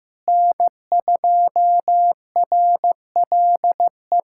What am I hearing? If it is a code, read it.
Morse code, heard as a keyed tone, about 15 wpm, 700 Hz. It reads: N2RLE
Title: USA Callsign DE Callsign - 5 Characters - 15wpm